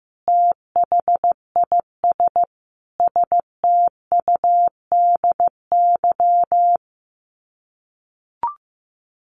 Morse code, heard as a keyed tone, about 15 wpm, 700 Hz. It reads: THIS STUDY
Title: Sets of 5 Words Encore2 15wpm